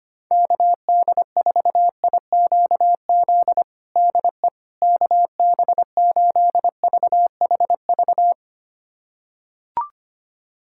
Morse code, heard as a keyed tone, about 25 wpm, 700 Hz. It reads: KD4IQZ DE KB8VHV
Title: USA Callsign DE Callsign - 6 Characters - 25wpm